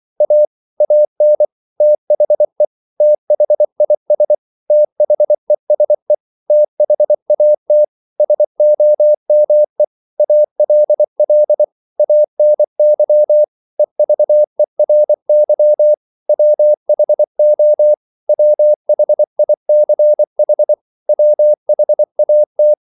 Morse code, heard as a keyed tone, about 24 wpm, 600 Hz. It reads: A AN THE THIS THESE THAT SOME ALL ANY EVERY WHO WHICH WHAT
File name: CW / Practice / 100MostCommonEnglishWords24